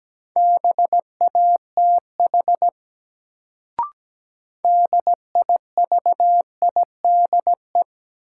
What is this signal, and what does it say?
Morse code, heard as a keyed tone, about 17 wpm, 700 Hz. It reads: BATH DIVIDE
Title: Sentences from Top 1600 Words - New Word Drill Alt - 17wpm